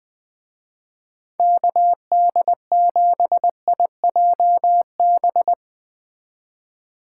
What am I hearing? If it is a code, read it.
Morse code, heard as a keyed tone, about 20 wpm, 700 Hz. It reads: KD7IJB
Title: Speed Racing - USA 2x3 Call Signs 17wpm